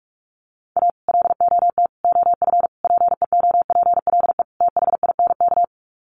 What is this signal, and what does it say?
Morse code, heard as a keyed tone, about 44 wpm, 700 Hz. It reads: A LOT OF PEOPLE THINK
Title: Sets of 5 Words Encore 45wpm